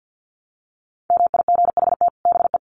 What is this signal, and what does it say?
Morse code, heard as a keyed tone, about 50 wpm, 700 Hz. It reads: MIGHT BE
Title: Sets of 2 Words 50wpm